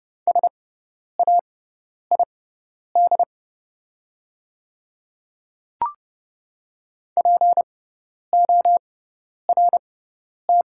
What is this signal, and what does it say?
Morse code, heard as a keyed tone, about 31 wpm, spaced as for 10 wpm, 700 Hz. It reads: SAID PORT
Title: 6x Character Spacing - Top 300 Words - 30wpm